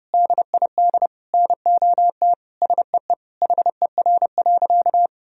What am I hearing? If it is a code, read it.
Morse code, heard as a keyed tone, about 30 wpm, 700 Hz. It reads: DID NOT SEE HER.
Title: Sentences from Top 1000 Words - Review Part 4 - 30wpm